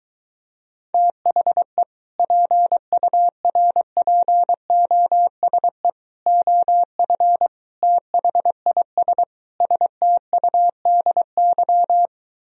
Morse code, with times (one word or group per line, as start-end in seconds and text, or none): 0.9-1.8 THE
2.2-5.9 PURPOSE
6.3-7.5 OF
7.8-9.2 THIS
9.6-12.1 STUDY